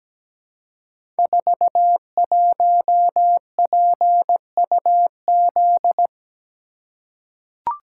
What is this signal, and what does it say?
Morse code, heard as a keyed tone, about 17 wpm, 700 Hz. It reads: 41PUZ
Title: Code Groups - 5 Letter:Numbers - 17wpm